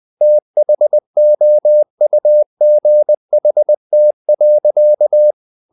Morse code, heard as a keyed tone, about 20 wpm, 600 Hz. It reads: THOUGHT.
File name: the old man and the sea part 2 20wpm